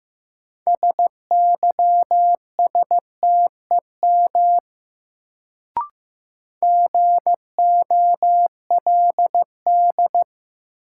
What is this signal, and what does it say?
Morse code, heard as a keyed tone, about 15 wpm, 700 Hz. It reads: SYSTEM GOLD
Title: Sentences from Top 1000 Words - Review Part 3 - New Word Drill - 15wpm